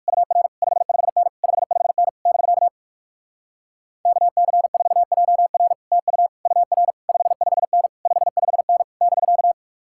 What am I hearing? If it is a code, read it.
Morse code, heard as a keyed tone, about 59 wpm, 700 Hz. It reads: UR 55N 55N <BK> KC4JR TU UR 55N 55N <BK>